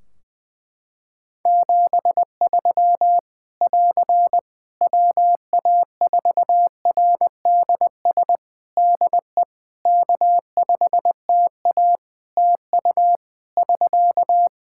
Morse code, heard as a keyed tone, about 20 wpm, 700 Hz. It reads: 73 <AR> WA4RDS DE K5TA TU <SK>
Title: Practice QSO - Exchange 4 - Ending - 20wpm